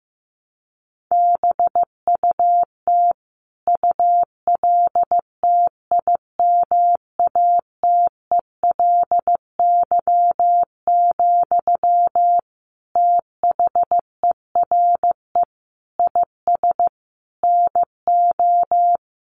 Morse code, heard as a keyed tone, about 15 wpm, 700 Hz. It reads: BUT ULTIMATELY, THERE IS NO
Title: Sentences from Top 1200 Words 15wpm